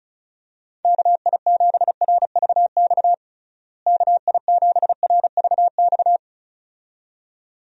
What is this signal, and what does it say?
Morse code, heard as a keyed tone, about 36 wpm, 700 Hz. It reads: KI7RVX KI7RVX
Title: USA Call Signs - 6 Characters - Repeated 2x - 35wpm